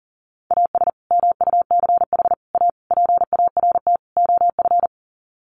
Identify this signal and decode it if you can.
Morse code, heard as a keyed tone, about 40 wpm, 700 Hz. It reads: AS MUCH A PART OF
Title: Sets of 5 Words Encore2 40wpm